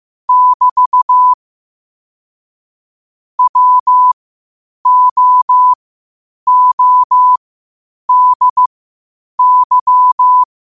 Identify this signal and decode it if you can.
Morse code, heard as a keyed tone, about 15 wpm, spaced as for 8 wpm, 1000 Hz. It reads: <BT> WOODY
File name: qotdfiles / 8wpm